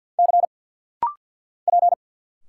Morse code, heard as a keyed tone, about 50 wpm, 690 Hz. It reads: C P
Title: Single Letters - Rapid-Fire 50wpm